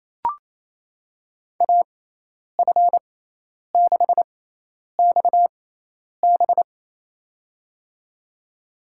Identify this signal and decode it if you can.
Morse code, heard as a keyed tone, about 27 wpm, spaced as for 10 wpm, 700 Hz. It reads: AF6XB
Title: USA Call Signs - 6x Character Spacing - 28wpm